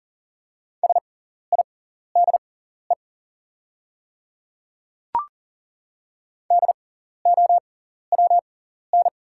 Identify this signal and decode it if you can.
Morse code, heard as a keyed tone, about 38 wpm, spaced as for 14 wpm, 690 Hz. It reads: SIDE DOWN
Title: 6x Character Spacing - Top 100 Words - 40wpm